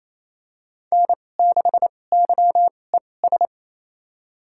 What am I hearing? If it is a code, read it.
Morse code, heard as a keyed tone, about 28 wpm, spaced as for 20 wpm, 700 Hz. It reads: N6YES
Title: USA Call Signs - 5 Characters - 2x Character Spacing - 28wpm